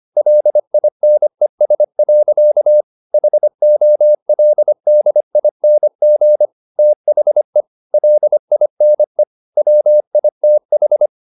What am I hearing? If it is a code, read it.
Morse code, heard as a keyed tone, about 25 wpm, 600 Hz. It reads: LINES. HOLDING THE LINE WITH
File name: the old man and the sea part 1 25wpm